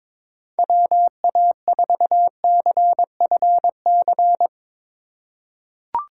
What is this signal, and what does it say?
Morse code, heard as a keyed tone, about 22 wpm, 700 Hz. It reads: WA4CFC
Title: USA Call Signs - 6 Characters - 8x Character Spacing - 22wpm